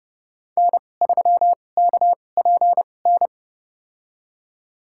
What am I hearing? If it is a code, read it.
Morse code, heard as a keyed tone, about 30 wpm, spaced as for 22 wpm, 700 Hz. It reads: N3KPN
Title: USA Call Signs - 5 Characters - 2x Character Spacing - 30wpm